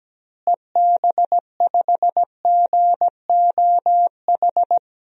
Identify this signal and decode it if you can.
Morse code, heard as a keyed tone, about 17 wpm, 700 Hz. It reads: EB5GOH
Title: DXCC-100 Call Signs - 6 Characters - Repeated 2x - 17wpm